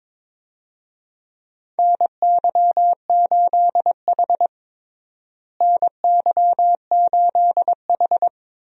Morse code, heard as a keyed tone, about 22 wpm, 700 Hz. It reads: NY8H NY8H
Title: USA Call Signs - 4 Characters - Repeated 3x - 22wpm